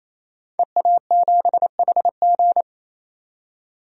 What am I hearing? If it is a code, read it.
Morse code, heard as a keyed tone, about 28 wpm, 700 Hz. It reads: EA7HG
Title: DXCC-100 Call Signs - 5 Characters - 28wpm